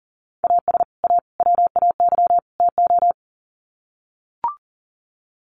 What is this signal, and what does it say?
Morse code, heard as a keyed tone, about 40 wpm, 700 Hz. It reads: AS A WAY TO
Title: Sets of 4 Words 40wpm